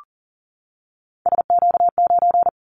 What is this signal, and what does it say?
Morse code, heard as a keyed tone, about 40 wpm, 700 Hz. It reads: SQ9
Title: Callsign Prefixes for Top 25 DXCC - 40wpm